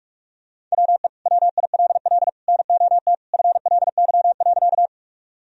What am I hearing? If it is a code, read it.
Morse code, heard as a keyed tone, about 46 wpm, 700 Hz. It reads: WE WILL NOT FLY.
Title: Sentences from Top 1000 Words - Review Part 2 - 45wpm